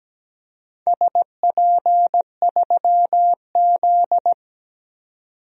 Morse code, heard as a keyed tone, about 17 wpm, 700 Hz. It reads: SP3Z
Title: DXCC-100 Call Signs - 4 Characters - 17wpm